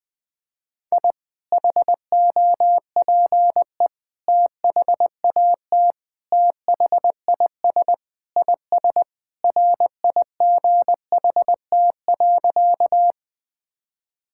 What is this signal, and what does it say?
Morse code, heard as a keyed tone, about 20 wpm, 700 Hz. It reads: I HOPE THAT THIS IS RIGHT.
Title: Sentences with 3-Word Phrases - Spoken Beginning - 20wpm